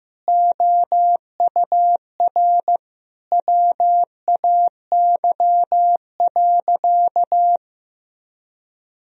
Morse code, heard as a keyed tone, about 15 wpm, 700 Hz. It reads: OUR WAY.
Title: Sentences from Top 1000 Words - Review Part 1 - 15wpm